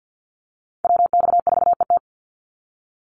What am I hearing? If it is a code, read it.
Morse code, heard as a keyed tone, about 48 wpm, 700 Hz. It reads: WX4ET
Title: NA SOTA Chaser Callsigns 50wpm